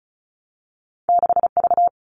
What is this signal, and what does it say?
Morse code, heard as a keyed tone, about 34 wpm, 700 Hz. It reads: 6V
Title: All Cut Numbers - 2-Digit 35wpm